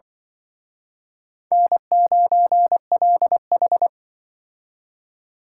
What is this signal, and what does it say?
Morse code, heard as a keyed tone, about 24 wpm, 690 Hz. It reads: N9LH
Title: Speed Racing - USA 1x2 Call Signs 20wpm